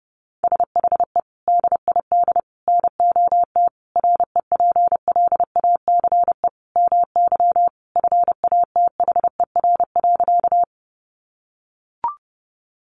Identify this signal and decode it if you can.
Morse code, heard as a keyed tone, about 30 wpm, 700 Hz. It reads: SHE DID NOT REPLACE MY FATHER.
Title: Sentences from Top 1400 Words 30wpm